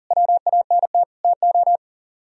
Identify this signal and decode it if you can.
Morse code, heard as a keyed tone, about 51 wpm, 700 Hz. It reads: WANT TO